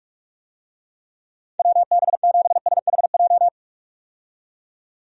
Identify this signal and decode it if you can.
Morse code, heard as a keyed tone, about 45 wpm, 700 Hz. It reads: WB7SHJ
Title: USA Call Signs - 6 Characters - 45wpm